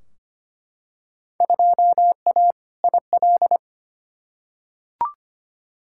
Morse code, heard as a keyed tone, about 25 wpm, 700 Hz. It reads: 2A IL
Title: Field Day Exchanges - 25wpm